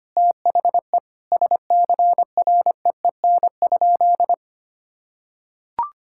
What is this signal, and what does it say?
Morse code, heard as a keyed tone, about 25 wpm, 700 Hz. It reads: THE SCREEN?
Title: Sentences from Top 1000 Words - Review Part 6 - 25wpm